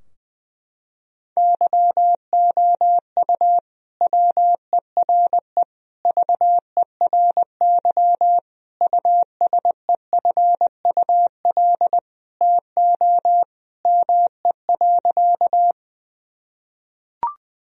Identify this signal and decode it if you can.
Morse code, heard as a keyed tone, about 20 wpm, 700 Hz. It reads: YOU WERE VERY USEFUL TO ME.
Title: Sentences from Top 1000 Words - Review Part 4 - 20wpm